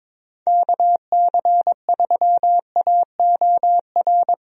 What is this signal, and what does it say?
Morse code, heard as a keyed tone, about 22 wpm, 700 Hz. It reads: KC3AOR
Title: USA Call Signs - 6 Characters - Repeated 2x - 22wpm